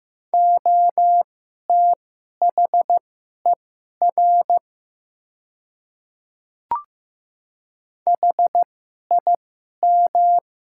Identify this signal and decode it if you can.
Morse code, heard as a keyed tone, about 15 wpm, spaced as for 11 wpm, 700 Hz. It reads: OTHER HIM
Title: 2x Character Spacing - Top 100 Words - 15wpm